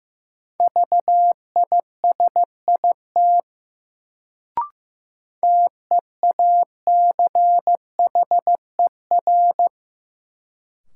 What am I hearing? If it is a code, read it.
Morse code, heard as a keyed tone, about 15 wpm, 700 Hz. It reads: VISIT TEACHER
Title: Sentences from Top 900 Words - New Word Drill - 15wpm